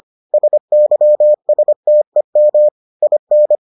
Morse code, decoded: SYSTEM IN

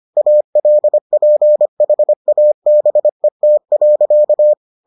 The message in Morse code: ALPHABET.

